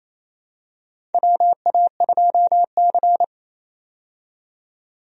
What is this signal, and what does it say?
Morse code, heard as a keyed tone, about 28 wpm, 700 Hz. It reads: WA2C
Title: USA Call Signs - 4 Characters - 28wpm